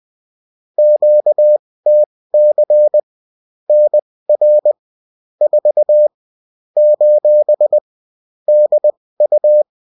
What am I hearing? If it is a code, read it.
Morse code, heard as a keyed tone, about 20 wpm, spaced as for 16 wpm, 600 Hz. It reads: QTC NR 4 : DU